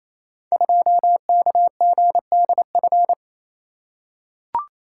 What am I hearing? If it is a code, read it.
Morse code, heard as a keyed tone, about 27 wpm, 700 Hz. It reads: 2KGDF
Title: Code Groups - 5 Letter:Numbers - 28wpm